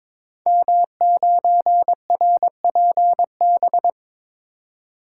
Morse code, heard as a keyed tone, about 22 wpm, 700 Hz. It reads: M9RPB
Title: Code Groups - 5 Letter:Numbers - 22wpm